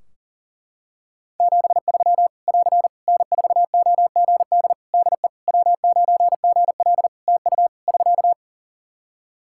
Morse code, heard as a keyed tone, about 40 wpm, 700 Hz. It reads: 73 <AR> N4OGD DE W9GL TU <SK>
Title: K1USN SST QSO Practice 40wpm